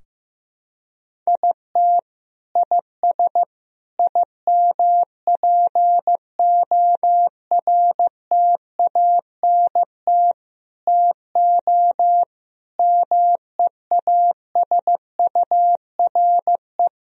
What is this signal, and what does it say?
Morse code, heard as a keyed tone, about 15 wpm, 700 Hz. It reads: IT IS IMPORTANT TO MEASURE
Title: Sentences from Top 1000 Words - Review Part 2 - 15wpm